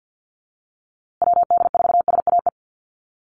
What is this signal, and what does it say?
Morse code, heard as a keyed tone, about 50 wpm, 700 Hz. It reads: WD4SAE
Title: USA Call Signs Repeated 3x 50wpm